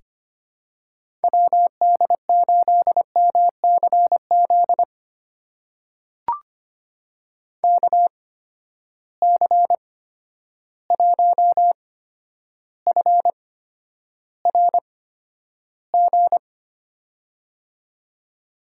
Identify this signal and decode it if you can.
Morse code, heard as a keyed tone, about 25 wpm, spaced as for 11 wpm, 700 Hz. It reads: WD8MCZ KC1FRG
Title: USA Call Signs - 6 Characters - 8x Character Spacing - 25wpm